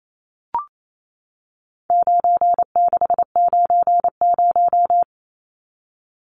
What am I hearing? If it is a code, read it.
Morse code, heard as a keyed tone, about 28 wpm, 700 Hz. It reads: 9690
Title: Numbers - 4-Digit 28wpm